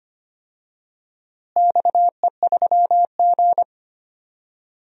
Morse code, heard as a keyed tone, about 25 wpm, 700 Hz. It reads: XE3G
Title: DXCC-100 Call Signs - 4 Characters - Repeated 2x - 25wpm